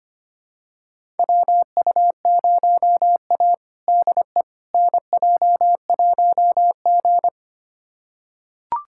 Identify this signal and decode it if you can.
Morse code, heard as a keyed tone, about 25 wpm, 700 Hz. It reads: WU0A DE NJ1G
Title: USA Callsign DE Callsign - 4 Characters - 25wpm